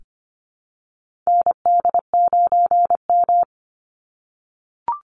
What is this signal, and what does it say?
Morse code, heard as a keyed tone, about 25 wpm, 700 Hz. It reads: ND9M
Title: Call Signs - 25wpm